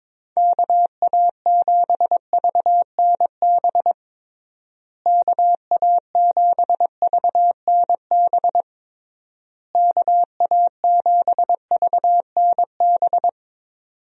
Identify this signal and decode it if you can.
Morse code, heard as a keyed tone, about 22 wpm, 700 Hz. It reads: KA7VNB KA7VNB KA7VNB
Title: USA Call Signs - 6 Characters - Repeated 3x - 22wpm